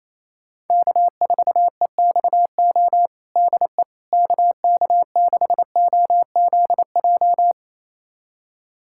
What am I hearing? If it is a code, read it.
Morse code, heard as a keyed tone, about 28 wpm, 700 Hz. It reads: K4EXO DE KK6OZJ
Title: USA Callsign DE Callsign 28wpm